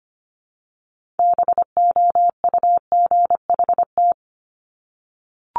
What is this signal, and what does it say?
Morse code, heard as a keyed tone, about 25 wpm, 700 Hz. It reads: BOUGHT
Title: Multi-letter Phonemes Repeated 3x 25wpm